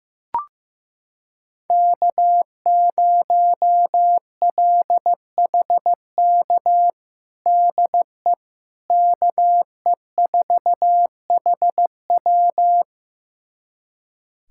Morse code, heard as a keyed tone, about 15 wpm, 700 Hz. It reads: K0LHK DE KE4HW
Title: USA Callsign DE Callsign - 5 Characters - 15wpm